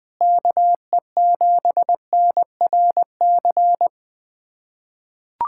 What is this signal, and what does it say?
Morse code, heard as a keyed tone, about 20 wpm, 700 Hz. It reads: KE7NRC
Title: USA Callsign DE Callsign - 6 Characters - 20wpm